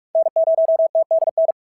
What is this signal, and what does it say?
Morse code, heard as a keyed tone, about 46 wpm, 690 Hz. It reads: N0TDN
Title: SOTA QSO Practice 45wpm